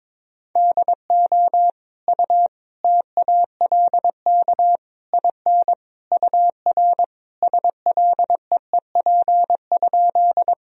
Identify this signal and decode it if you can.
Morse code, heard as a keyed tone, about 22 wpm, 700 Hz. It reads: DO U TALK IN UR SLEEP?
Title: Sentences from Top 500 Words and Abbreviations 22wpm